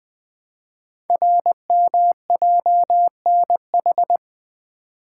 Morse code, heard as a keyed tone, about 20 wpm, 700 Hz. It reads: RMJNH
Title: Code Groups - 5 Letters - 20wpm